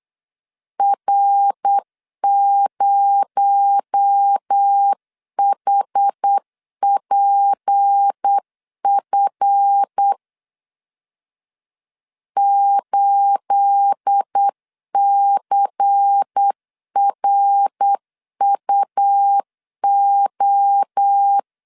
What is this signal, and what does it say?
Morse code, heard as a keyed tone, about 8 wpm, 800 Hz. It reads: R0HPF 8CRUO
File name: LEZIONI / 9 28 8wpm 3512